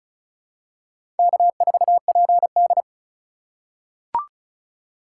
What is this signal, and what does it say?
Morse code, heard as a keyed tone, about 34 wpm, 700 Hz. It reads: K4PD
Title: USA Call Signs - 4 Characters - 6x Character Spacing - 35wpm